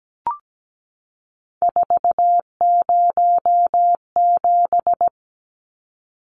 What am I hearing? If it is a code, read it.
Morse code, heard as a keyed tone, about 17 wpm, 700 Hz. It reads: 407
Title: Numbers - 3-Digit 17wpm